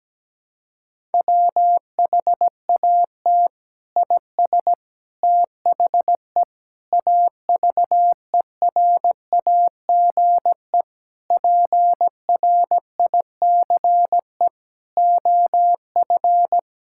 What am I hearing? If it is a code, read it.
Morse code, heard as a keyed tone, about 17 wpm, 700 Hz. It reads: WHAT IS THE AVERAGE PRICE OF
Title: Sentences from Top 1000 Words - Review Part 6 - 17wpm